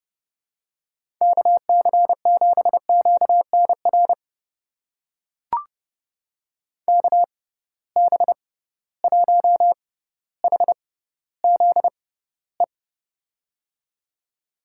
Morse code, heard as a keyed tone, about 30 wpm, spaced as for 15 wpm, 700 Hz. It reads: KC7QNR KB1HZE
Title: USA Call Signs - 6 Characters - 6x Character Spacing - 30wpm